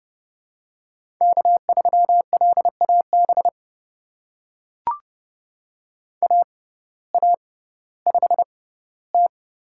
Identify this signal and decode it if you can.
Morse code, heard as a keyed tone, about 31 wpm, spaced as for 17 wpm, 700 Hz. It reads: K3LAB AA5T
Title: USA Call Signs - 6x Character Spacing - 30wpm